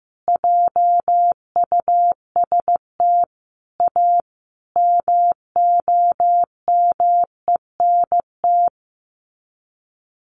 Morse code, heard as a keyed tone, about 14 wpm, 700 Hz. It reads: JUST A MOMENT
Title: Sets of 5 Words Encore 15wpm